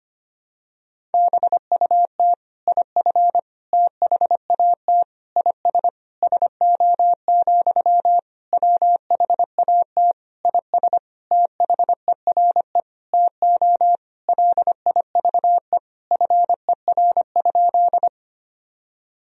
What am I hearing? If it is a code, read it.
Morse code, heard as a keyed tone, about 25 wpm, 700 Hz. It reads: BUT IF THAT IS SO, WHAT IS THERE TO LIVE FER?
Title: Sentences from Top 200 Words and Abbreviations 25wpm